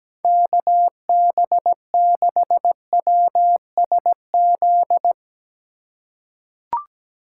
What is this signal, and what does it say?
Morse code, heard as a keyed tone, about 17 wpm, 700 Hz. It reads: KB6WSZ
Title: USA Call Signs - 6 Characters - 2x Character Spacing - 17wpm